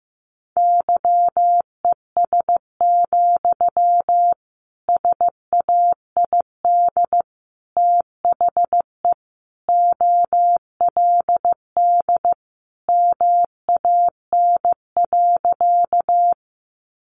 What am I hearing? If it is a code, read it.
Morse code, heard as a keyed tone, about 15 wpm, 700 Hz. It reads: YES, SAID THE OLD MAN.
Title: Sentences from Top 500 Words 15wpm